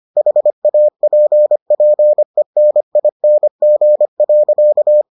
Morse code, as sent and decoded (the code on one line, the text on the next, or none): .... .- .--. .--. -. .. -. --. .-.-.-
HAPPENING.